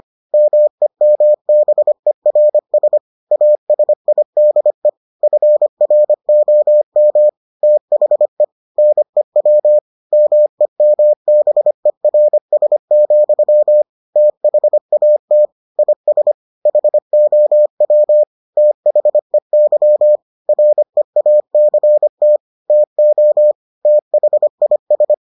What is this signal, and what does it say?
Morse code, heard as a keyed tone, about 25 wpm, 600 Hz. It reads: MEMBERS ASIDE FROM THE NEW MEMBERS, THAT IS HOW THEY REACT TO THIS